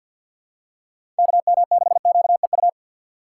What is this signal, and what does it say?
Morse code, heard as a keyed tone, about 53 wpm, 700 Hz. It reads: KM6QEU
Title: USA Call Signs - 6 Characters - 8x Character Spacing - 50wpm